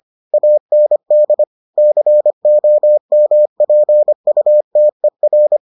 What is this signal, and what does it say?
Morse code, heard as a keyed tone, about 25 wpm, 600 Hz. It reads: AND COMPUTER